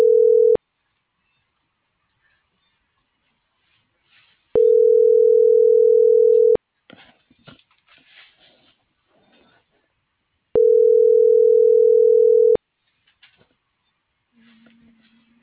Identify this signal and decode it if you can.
Ringback tone (North America), 440 + 480 Hz.